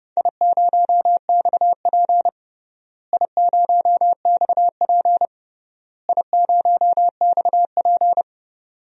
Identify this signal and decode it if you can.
Morse code, heard as a keyed tone, about 32 wpm, 700 Hz. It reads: I0XP I0XP I0XP
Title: DXCC-100 Call Signs - 4 Characters - Repeated 3x - 30wpm